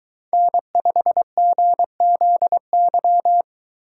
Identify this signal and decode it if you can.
Morse code, heard as a keyed tone, about 23 wpm, 700 Hz. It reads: N5GZY